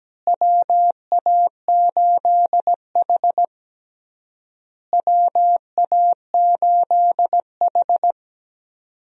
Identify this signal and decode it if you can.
Morse code, heard as a keyed tone, about 17 wpm, 700 Hz. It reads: WA8H WA8H
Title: USA Call Signs - 4 Characters - Repeated 3x - 17wpm